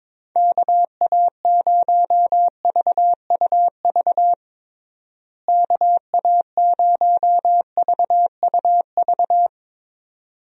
Morse code, heard as a keyed tone, about 22 wpm, 700 Hz. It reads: KA0VUV KA0VUV
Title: USA Call Signs - 6 Characters - Repeated 2x - 22wpm